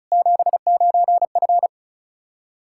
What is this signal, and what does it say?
Morse code, heard as a keyed tone, about 35 wpm, 700 Hz. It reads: 79F
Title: Practice QSO - Exchange 2 - Info - 35wpm